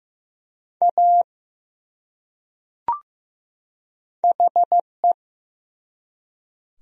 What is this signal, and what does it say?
Morse code, heard as a keyed tone, about 15 wpm, 700 Hz. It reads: A HE
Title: Top 100 Words 15wpm